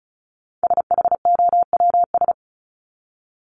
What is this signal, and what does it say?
Morse code, heard as a keyed tone, about 34 wpm, 700 Hz. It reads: SHOWS
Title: ES:S Suffix - Top 100 Words with Base Words 35wpm